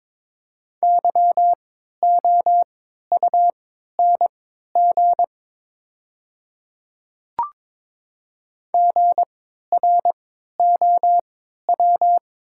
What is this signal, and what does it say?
Morse code, heard as a keyed tone, about 22 wpm, spaced as for 12 wpm, 700 Hz. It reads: YOUNG GROW
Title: 3x Character Spacing - Top 500 Words - 22wpm